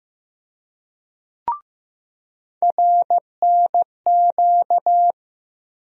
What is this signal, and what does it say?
Morse code, heard as a keyed tone, about 15 wpm, 700 Hz. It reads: RNQ
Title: USA Call Sign Postfixes - 3 Letters 15wpm